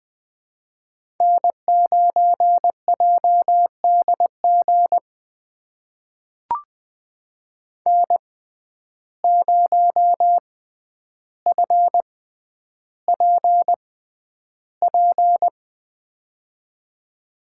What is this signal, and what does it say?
Morse code, heard as a keyed tone, about 21 wpm, spaced as for 10 wpm, 700 Hz. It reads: N9JDG N0FPP
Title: USA Call Signs - 5 Characters - 6x Character Spacing - 20wpm